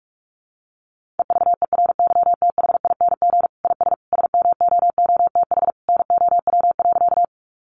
Morse code, heard as a keyed tone, about 44 wpm, 700 Hz. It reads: EVERYTHING IS SMOOTH NOW.
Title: Sentences from Top 1400 Words 45wpm